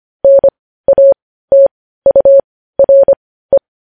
Morse code, heard as a keyed tone, about 25 wpm, spaced as for 15 wpm, 550 Hz. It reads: NATURE
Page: news-headlines-in-morse-code-at-15-wpm